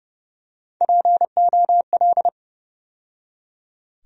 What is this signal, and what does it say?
Morse code, heard as a keyed tone, about 30 wpm, 700 Hz. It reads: POL
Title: USA Call Sign Postfixes - 3 Letters 30wpm